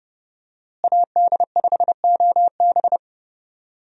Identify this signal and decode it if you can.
Morse code, heard as a keyed tone, about 30 wpm, 700 Hz. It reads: AD5OB
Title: USA Call Signs - 5 Characters - Repeated 2x - 30wpm